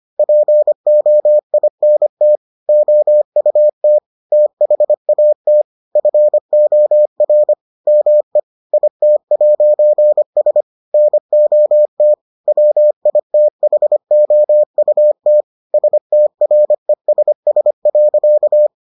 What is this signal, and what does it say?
Morse code, heard as a keyed tone, about 25 wpm, 600 Hz. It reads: POINT OUT THAT FOR ME IT'S NOT WITHOUT STRESS.